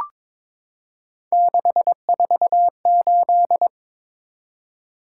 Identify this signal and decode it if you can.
Morse code, heard as a keyed tone, about 22 wpm, 700 Hz. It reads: 648
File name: practice / 22wpm / Three Letter-Number Combinations - 22wpm